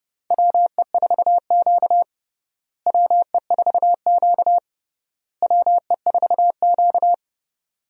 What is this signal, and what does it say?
Morse code, heard as a keyed tone, about 30 wpm, 700 Hz. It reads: WE4Q WE4Q WE4Q
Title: USA Call Signs - 4 Characters - Repeated 3x - 30wpm